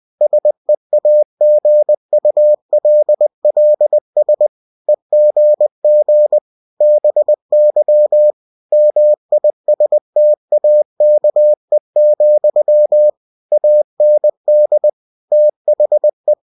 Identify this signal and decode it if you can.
Morse code, heard as a keyed tone, about 20 wpm, 600 Hz. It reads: SEAGULLS EGG BY MISTAKE, AND THE
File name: cw-books / winnie-the-pooh / winnie-20wpm-page-0101